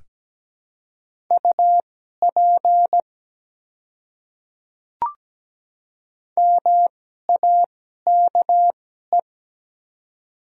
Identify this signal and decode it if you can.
Morse code, heard as a keyed tone, about 17 wpm, spaced as for 12 wpm, 690 Hz. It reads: UP MAKE
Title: 2x Character Spacing - Top 100 Words - 17wpm